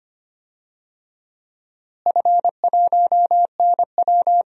Morse code, heard as a keyed tone, about 25 wpm, 700 Hz. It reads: F1NW
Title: DXCC-100 Call Signs - 4 Characters - Repeated 2x - 25wpm